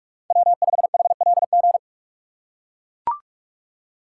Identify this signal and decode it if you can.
Morse code, heard as a keyed tone, about 44 wpm, 700 Hz. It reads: W5HLG